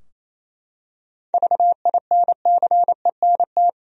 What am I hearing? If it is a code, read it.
Morse code, heard as a keyed tone, about 28 wpm, 700 Hz. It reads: VINCENT
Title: USA Common Names - 28wpm